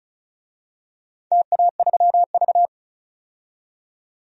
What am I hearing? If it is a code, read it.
Morse code, heard as a keyed tone, about 34 wpm, 700 Hz. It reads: TA3V
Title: DXCC-100 Call Signs - 4 Characters - 35wpm